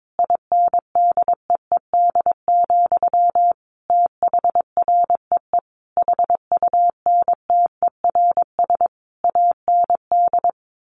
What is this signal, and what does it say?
Morse code, heard as a keyed tone, about 22 wpm, 700 Hz. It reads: INDEED, THREE HUNTERS AND
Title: A Sound Of Thunder 22wpm